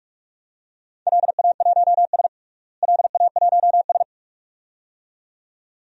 Morse code, heard as a keyed tone, about 46 wpm, 700 Hz. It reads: LA1S LA1S
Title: DXCC-100 Call Signs - 4 Characters - Repeated 2x - 45wpm